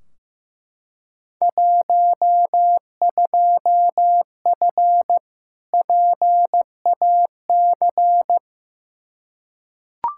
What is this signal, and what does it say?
Morse code, heard as a keyed tone, about 15 wpm, 700 Hz. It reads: 12F PAC
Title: Field Day Exchanges Encore - 15wpm